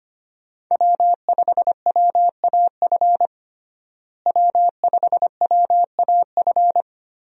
Morse code, heard as a keyed tone, about 25 wpm, 700 Hz. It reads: W5WAF W5WAF
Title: USA Call Signs - 5 Characters - Repeated 3x - 25wpm